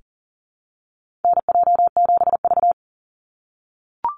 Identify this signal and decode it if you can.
Morse code, heard as a keyed tone, about 40 wpm, 700 Hz. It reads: NJ7V
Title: NA SOTA Activator Callsigns 40wpm